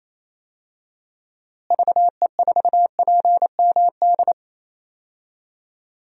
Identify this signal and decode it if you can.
Morse code, heard as a keyed tone, about 28 wpm, 700 Hz. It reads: VE4PMD
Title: DXCC-100 Call Signs - 6 Characters - Repeated 3x - 28wpm